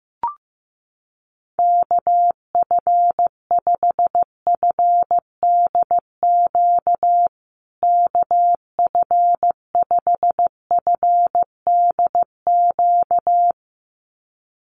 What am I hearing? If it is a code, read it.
Morse code, heard as a keyed tone, about 15 wpm, 700 Hz. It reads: KF5FDQ KF5FDQ
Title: USA Call Signs Repeated 2x 15wpm